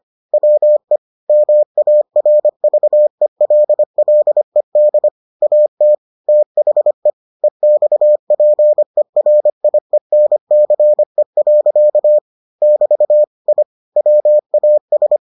Morse code, heard as a keyed tone, about 25 wpm, 600 Hz. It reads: WE MARVELLED AT THE EXPERIENCE. <BT> I WAS